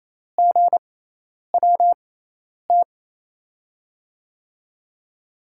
Morse code, heard as a keyed tone, about 28 wpm, spaced as for 10 wpm, 700 Hz. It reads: GWT
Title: USA Call Signs - 5 Characters - 6x Character Spacing - 28wpm